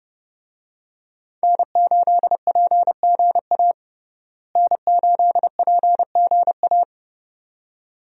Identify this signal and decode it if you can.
Morse code, heard as a keyed tone, about 30 wpm, 700 Hz. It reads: N8PGA N8PGA
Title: USA Call Signs - 5 Characters - Repeated 2x - 30wpm